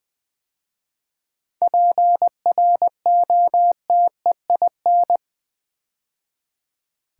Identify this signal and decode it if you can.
Morse code, heard as a keyed tone, about 20 wpm, 700 Hz. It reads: PROTEIN
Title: 7 Letter Words - First 3 Letters Given - 20wpm